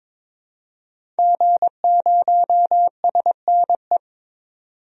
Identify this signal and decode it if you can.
Morse code, heard as a keyed tone, about 22 wpm, 700 Hz. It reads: G0SNE
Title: DXCC-100 Call Signs - 5 Characters - 22wpm